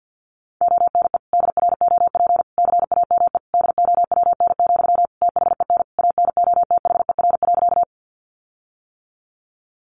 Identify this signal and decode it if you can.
Morse code, heard as a keyed tone, about 48 wpm, 700 Hz. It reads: ONE DROP CAME DOWN, THEN ANOTHER.
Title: Sentences from Top 1400 Words 50wpm